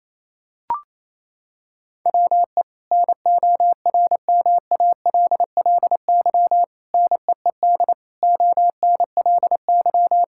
Morse code, heard as a keyed tone, about 27 wpm, 700 Hz. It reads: WE NORMALLY NEED ONLY
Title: Sentences from Top 1000 Words - Review Part 6 - 28wpm